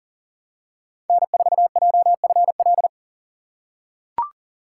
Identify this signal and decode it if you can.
Morse code, heard as a keyed tone, about 40 wpm, 700 Hz. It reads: N4JFL
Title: USA Call Signs - 5 Characters - 6x Character Spacing - 40wpm